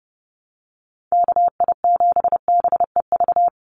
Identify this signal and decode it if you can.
Morse code, heard as a keyed tone, about 30 wpm, 700 Hz. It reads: KI7BEV
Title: USA Call Signs 30wpm